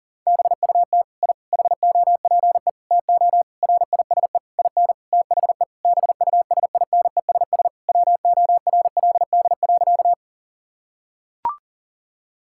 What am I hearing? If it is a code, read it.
Morse code, heard as a keyed tone, about 40 wpm, 700 Hz. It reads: BUT I HOPE TO RISE IN THE BUSINESS WORLD.
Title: Sentences from Top 1000 Words - Review Part 6 - 40wpm